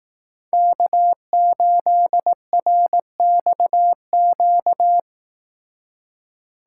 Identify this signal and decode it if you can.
Morse code, heard as a keyed tone, about 18 wpm, 700 Hz. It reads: K8RXQ
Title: Speed Racing - USA 1x3 Call Signs 15wpm